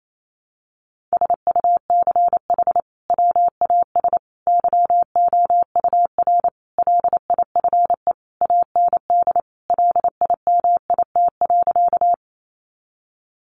Morse code, heard as a keyed tone, about 28 wpm, 700 Hz. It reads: SUCH WAS YOUR LIFE AND LIMIT.
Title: Sentences from Top 1500 Words 28wpm